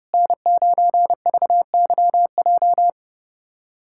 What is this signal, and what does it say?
Morse code, heard as a keyed tone, about 30 wpm, 700 Hz. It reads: N9VYJ
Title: USA Callsign DE Callsign - 5 Characters - 30wpm